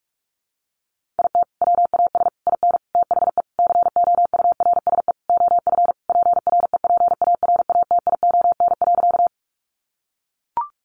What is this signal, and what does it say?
Morse code, heard as a keyed tone, about 44 wpm, 700 Hz. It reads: IT WAS IN THE COURSE OF PREPARATION.
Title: Sentences from Top 1400 Words 45wpm